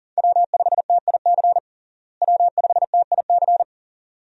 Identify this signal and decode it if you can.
Morse code, heard as a keyed tone, about 37 wpm, 700 Hz. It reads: W5TIC W5TIC
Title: USA Call Signs - 5 Characters - Repeated 2x - 40wpm